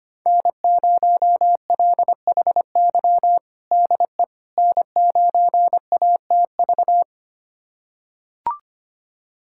Morse code, heard as a keyed tone, about 25 wpm, 700 Hz. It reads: N0LHY DE N9ATV
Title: USA Callsign DE Callsign - 5 Characters - 25wpm